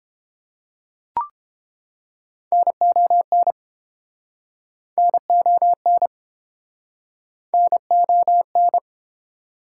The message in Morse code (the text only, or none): NON NON NON